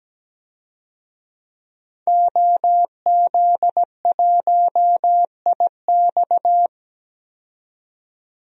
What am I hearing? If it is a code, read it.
Morse code, heard as a keyed tone, about 17 wpm, 700 Hz. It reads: OZ1IX
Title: DXCC-100 Call Signs - 5 Characters - 17wpm